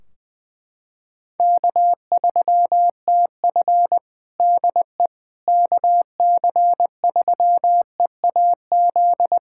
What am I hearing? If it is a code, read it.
Morse code, heard as a keyed tone, about 21 wpm, 700 Hz. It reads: K3TF DE KC3EAZ
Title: Practice QSO - Exchange 4 - Ending - 20wpm